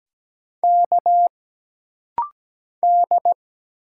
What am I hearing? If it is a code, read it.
Morse code, heard as a keyed tone, about 17 wpm, 700 Hz. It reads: K D
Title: Single Letter-Number - Rapid-Fire 17wpm